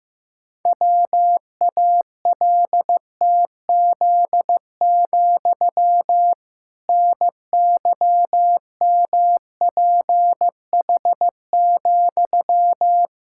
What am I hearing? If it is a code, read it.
Morse code, heard as a keyed tone, about 15 wpm, 700 Hz. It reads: WALTZ, NYMPH,
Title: Pangram Challenge v2 - 15wpm